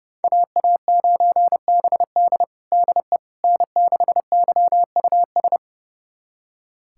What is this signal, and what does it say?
Morse code, heard as a keyed tone, about 30 wpm, 700 Hz. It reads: AA9BD DE N6YUS
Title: USA Callsign DE Callsign 30wpm